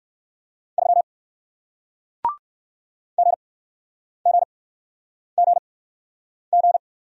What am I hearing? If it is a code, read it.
Morse code, heard as a keyed tone, about 60 wpm, spaced as for 9 wpm, 700 Hz. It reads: 3 GGGG